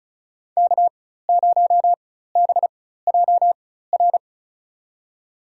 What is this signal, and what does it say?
Morse code, heard as a keyed tone, about 36 wpm, spaced as for 16 wpm, 700 Hz. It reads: K0BJR
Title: USA Call Signs - 5 Characters - 4x Character Spacing - 35wpm